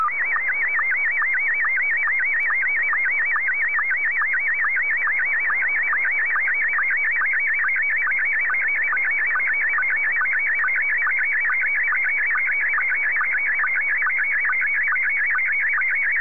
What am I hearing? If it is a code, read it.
SSTV (slow-scan television), Scottie 1.